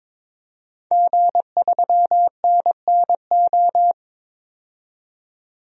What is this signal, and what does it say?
Morse code, heard as a keyed tone, about 22 wpm, 700 Hz. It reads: G3NNO
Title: DXCC-100 Call Signs - 5 Characters - 22wpm